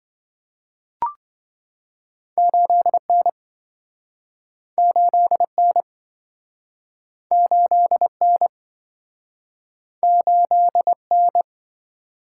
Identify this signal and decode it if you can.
Morse code, heard as a keyed tone, about 27 wpm, 700 Hz. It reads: 8N 8N 8N 8N